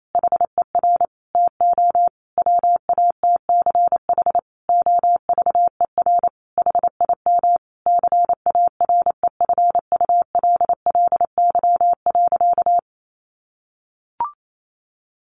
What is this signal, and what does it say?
Morse code, heard as a keyed tone, about 28 wpm, 700 Hz. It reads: HER TO WATCH OVER HIM CAREFULLY.
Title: Sentences from Top 1500 Words 28wpm